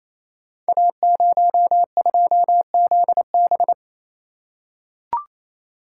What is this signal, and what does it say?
Morse code, heard as a keyed tone, about 29 wpm, 700 Hz. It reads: A02ZB
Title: Code Groups - 5 Letter:Numbers - 28wpm